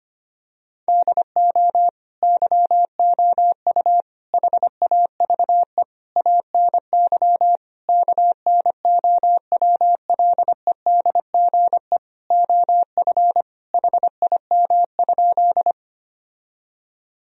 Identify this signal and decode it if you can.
Morse code, heard as a keyed tone, about 25 wpm, 700 Hz. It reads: DO YOU HAVE ANY KNOWLEDGE OF HIM?
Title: Sentences from Top 1000 Words - Review Part 3 - 25wpm